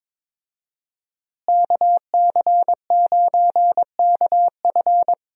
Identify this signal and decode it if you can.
Morse code, heard as a keyed tone, about 22 wpm, 700 Hz. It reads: KC9KF
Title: USA Call Signs - 5 Characters - Repeated 2x - 22wpm